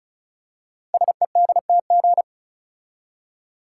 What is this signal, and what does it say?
Morse code, heard as a keyed tone, about 36 wpm, 700 Hz. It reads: SEDTG
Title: Code Groups - 5 Letter:Numbers - 35wpm